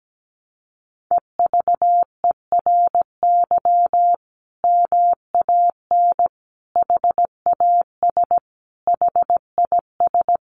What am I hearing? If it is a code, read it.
Morse code, heard as a keyed tone, about 17 wpm, 700 Hz. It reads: EVERY MAN HAS HIS
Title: Sentences from Top 600 Words 17wpm